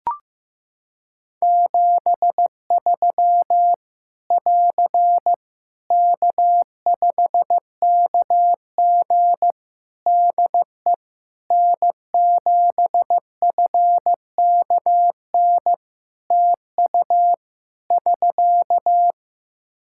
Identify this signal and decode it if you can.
Morse code, heard as a keyed tone, about 15 wpm, 700 Hz. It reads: 73 <AR> K5KG DE N7FKN TU <SK>
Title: K1USN SST QSO Practice 15wpm